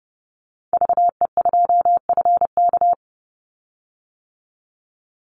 Morse code, heard as a keyed tone, about 30 wpm, 700 Hz. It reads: VE2FK
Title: Call Signs - 30wpm